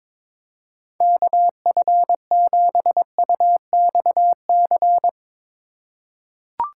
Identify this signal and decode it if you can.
Morse code, heard as a keyed tone, about 22 wpm, 700 Hz. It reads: KF7UXC
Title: K1USN SST QSO Practice - 22wpm